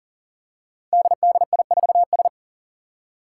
Morse code, heard as a keyed tone, about 44 wpm, 700 Hz. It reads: DDI4S
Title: Five Challenging Letter-Number Combinations 40wpm